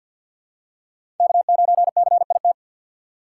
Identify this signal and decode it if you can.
Morse code, heard as a keyed tone, about 48 wpm, 700 Hz. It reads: K9CIT
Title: USA Call Signs - 5 Characters - 6x Character Spacing - 50wpm